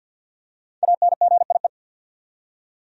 Morse code, heard as a keyed tone, about 46 wpm, 700 Hz. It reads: ANGIE
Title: USA Common Names - Max 5 Letters - 50wpm